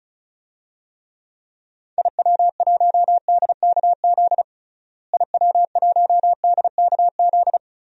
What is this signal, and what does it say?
Morse code, heard as a keyed tone, about 34 wpm, 700 Hz. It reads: IW1DKZ IW1DKZ
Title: DXCC-100 Call Signs - 6 Characters - Repeated 3x - 35wpm